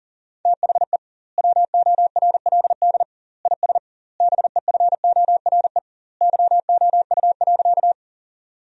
Morse code, heard as a keyed tone, about 40 wpm, 700 Hz. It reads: THE WORLD IS BEFORE YOU.
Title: 2x Word-Spacing - Sentences from Top 200 Words - 40wpm